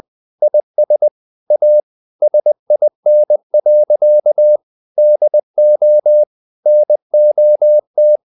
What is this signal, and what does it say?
Morse code, heard as a keyed tone, about 20 wpm, 600 Hz. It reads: IS A SIN. DO NOT
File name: the old man and the sea part 2 20wpm